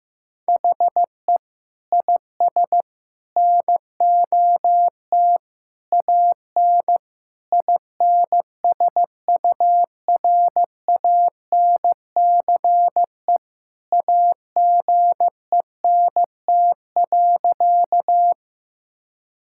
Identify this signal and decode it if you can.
Morse code, heard as a keyed tone, about 15 wpm, 700 Hz. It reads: HE IS NOT AN INSURANCE AGENT.
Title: Sentences from Top 1600 Words 15wpm